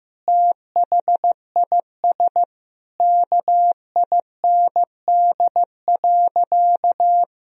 Morse code, heard as a keyed tone, about 15 wpm, 700 Hz. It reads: THIS KIND.
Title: Sentences from Top 1000 Words - Review Part 4 - 15wpm